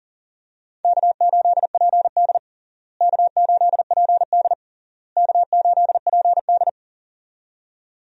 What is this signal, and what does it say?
Morse code, heard as a keyed tone, about 37 wpm, 700 Hz. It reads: K8PD K8PD K8PD
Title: USA Call Signs - 4 Characters - Repeated 3x - 40wpm